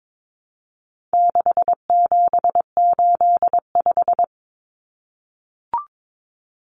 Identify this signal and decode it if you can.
Morse code, heard as a keyed tone, about 22 wpm, 700 Hz. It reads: 6785
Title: Common Cut Numbers - 4-Digit 22wpm